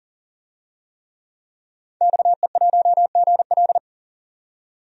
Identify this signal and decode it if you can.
Morse code, heard as a keyed tone, about 39 wpm, 700 Hz. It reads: XE1GL
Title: DXCC-100 Call Signs - 5 Characters - 40wpm